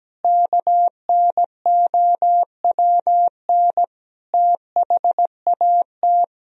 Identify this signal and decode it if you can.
Morse code, heard as a keyed tone, about 17 wpm, 700 Hz. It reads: KNOWN THAT
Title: Sentences from Top 1000 Words - Review Part 3 - 17wpm